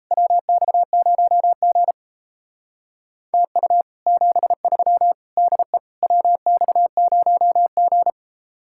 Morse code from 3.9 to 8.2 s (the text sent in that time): 73 DE WX0G